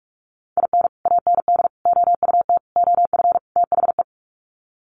Morse code, heard as a keyed tone, about 44 wpm, 700 Hz. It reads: IN AND OUT OF THE
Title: Sets of 5 Words 45wpm